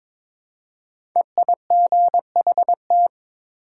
Morse code, heard as a keyed tone, about 22 wpm, 700 Hz. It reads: EIGHT
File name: 5 Letter Words - First 3 Letters Given - 22wpm